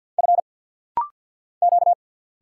Morse code, heard as a keyed tone, about 48 wpm, 700 Hz. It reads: F Q
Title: Single Letters - Rapid-Fire 50wpm